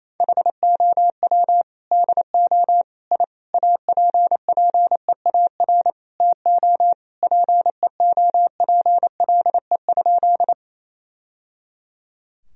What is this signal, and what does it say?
Morse code, heard as a keyed tone, about 27 wpm, 700 Hz. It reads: HOW DO I APPEAR TO PEOPLE?
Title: Sentences from Top 1000 Words - Review Part 2 - 28wpm